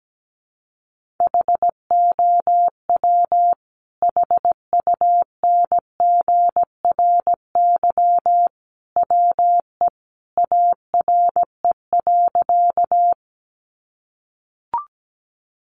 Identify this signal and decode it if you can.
Morse code, heard as a keyed tone, about 17 wpm, 700 Hz. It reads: HOW HUNGRY WE ARE.
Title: Sentences from Top 1400 Words 17wpm